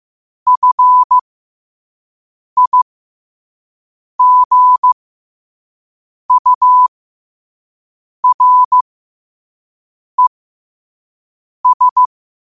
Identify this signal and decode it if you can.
Morse code, heard as a keyed tone, about 15 wpm, spaced as for 5 wpm, 1000 Hz. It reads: FIGURES
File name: qotdfiles / 5wpm